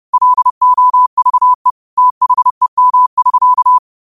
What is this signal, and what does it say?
Morse code, heard as a keyed tone, about 30 wpm, 1000 Hz. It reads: LOVE THEM<SK>
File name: qotdfiles / 30wpm